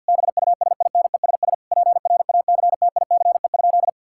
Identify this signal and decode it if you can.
Morse code, heard as a keyed tone, about 50 wpm, 700 Hz. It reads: BUSINESS PRACTICE?
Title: Sentences from Top 1000 Words - Review Part 6 - 50wpm